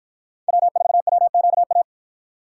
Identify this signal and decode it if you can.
Morse code, heard as a keyed tone, about 53 wpm, 700 Hz. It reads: W4WQA